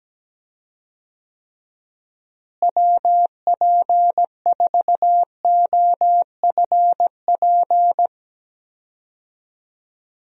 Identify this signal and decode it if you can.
Morse code, heard as a keyed tone, about 17 wpm, 700 Hz. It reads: WP4OFP
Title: USA Call Signs - 6 Characters - Repeated 2x - 17wpm